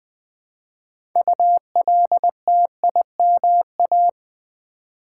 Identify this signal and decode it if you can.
Morse code, heard as a keyed tone, about 20 wpm, 700 Hz. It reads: ULTIMA
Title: Build A Word - 3 Letters At A Time - 8-Letter Words - 20wpm